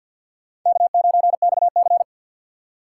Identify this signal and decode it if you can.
Morse code, heard as a keyed tone, about 53 wpm, 700 Hz. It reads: K9XC
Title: USA Call Signs - 4 Characters - 6x Character Spacing - 50wpm